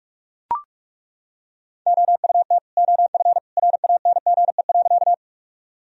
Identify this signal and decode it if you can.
Morse code, heard as a keyed tone, about 41 wpm, 700 Hz. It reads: OUT OF RANGE.
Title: Sentences from Top 1000 Words - Review Part 4 - 45wpm